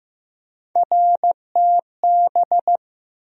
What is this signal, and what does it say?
Morse code, heard as a keyed tone, about 15 wpm, 700 Hz. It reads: RTB
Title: USA Call Sign Postfixes - 3 Letters 15wpm